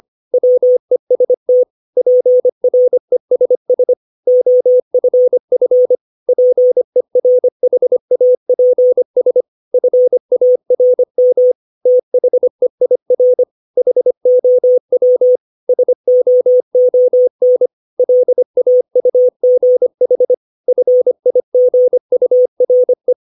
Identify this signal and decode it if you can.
Morse code, heard as a keyed tone, about 25 wpm, 500 Hz. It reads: WEST PRESS OFF PERHAPS FARM THEIR HOW SOON LAUGH FIGURE
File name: 632 words / 632 Words 25WPM